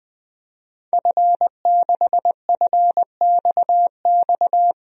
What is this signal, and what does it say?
Morse code, heard as a keyed tone, about 20 wpm, 700 Hz. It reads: F6FXX
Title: DXCC-100 Call Signs - 5 Characters - Repeated 3x - 20wpm